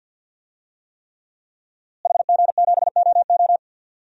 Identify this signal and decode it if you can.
Morse code, heard as a keyed tone, about 49 wpm, 700 Hz. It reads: HG7OO
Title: DXCC-100 Call Signs - 5 Characters - Repeated 3x - 50wpm